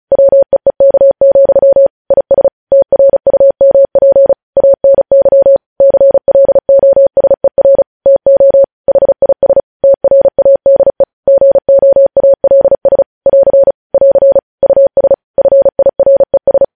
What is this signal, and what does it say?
Morse code, heard as a keyed tone, about 36 wpm, 550 Hz. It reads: WEEK, IS TRUMP ANY CLOSER TO HIS TRADE GOALS <AR> <AR> US FIRES
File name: podcast / News headlines-1x-35x35